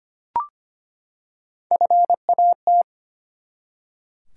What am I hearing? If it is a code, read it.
Morse code, heard as a keyed tone, about 25 wpm, 700 Hz. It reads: FAT
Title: Three Letter Words - 25wpm